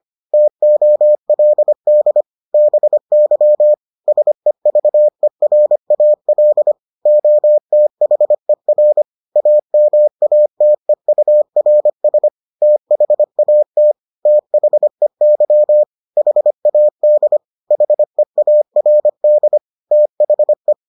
Morse code, heard as a keyed tone, about 25 wpm, 600 Hz. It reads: TOLD BY SEVERAL OTHER AMATEURS THAT THEY HAD HEARD THE